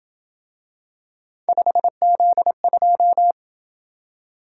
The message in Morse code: 5Z2

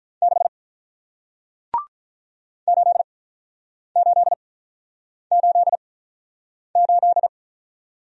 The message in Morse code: 6 8888

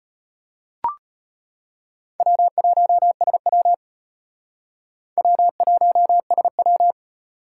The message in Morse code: W1SW W1SW

